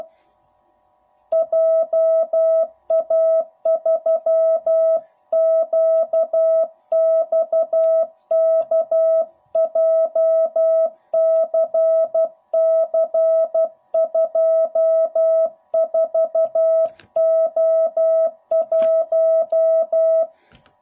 Morse code, heard as a keyed tone, about 12 wpm, 660 Hz. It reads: JA3QXK JCC24O1
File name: jh8cbh / 2021cw / cw05b